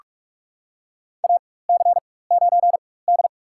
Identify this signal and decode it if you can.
Morse code, heard as a keyed tone, about 46 wpm, spaced as for 21 wpm, 700 Hz. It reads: AC9D
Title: USA Call Signs - 4x Character Spacing - 45wpm